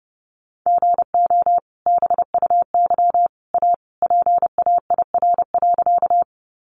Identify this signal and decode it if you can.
Morse code, heard as a keyed tone, about 30 wpm, 700 Hz. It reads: GO BUY A PAIR.
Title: Sentences from Top 1200 Words 30wpm